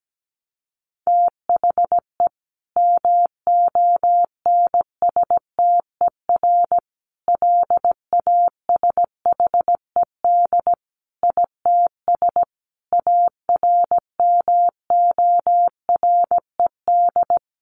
Morse code, heard as a keyed tone, about 17 wpm, 700 Hz. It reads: THE MONSTER LASHED ITS ARMORED
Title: A Sound Of Thunder 17wpm